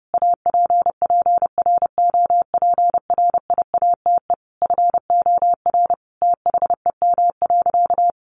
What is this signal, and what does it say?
Morse code, heard as a keyed tone, about 30 wpm, 700 Hz. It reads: APPROPRIATE FOR THEM.
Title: Sentences from Top 900 Words 30wpm